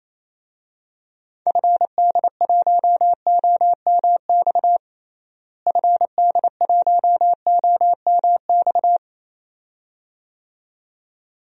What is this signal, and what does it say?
Morse code, heard as a keyed tone, about 27 wpm, 700 Hz. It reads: FD1OMX FD1OMX
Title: DXCC-100 Call Signs - 6 Characters - Repeated 2x - 28wpm